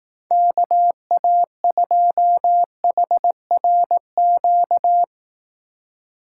Morse code, heard as a keyed tone, about 18 wpm, 700 Hz. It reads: KA2HRQ
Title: Speed Racing - USA 2x3 Call Signs 15wpm